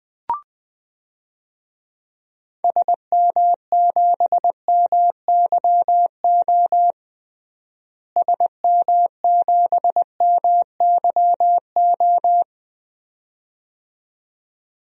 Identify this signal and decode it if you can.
Morse code, heard as a keyed tone, about 20 wpm, 700 Hz. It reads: SM7MYO SM7MYO
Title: DXCC-100 Call Signs - 6 Characters - Repeated 2x - 20wpm